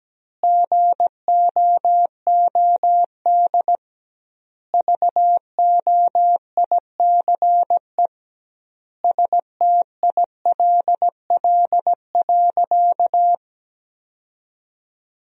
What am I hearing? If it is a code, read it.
Morse code, heard as a keyed tone, about 17 wpm, 700 Hz. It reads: GOOD VOICE STILL.
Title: 2x Word-Spacing - Sentences from Top 500 Words - 17wpm